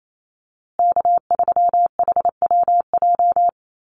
Morse code, heard as a keyed tone, about 28 wpm, 700 Hz. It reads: K3HWJ
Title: USA Call Signs Repeated 3x 28wpm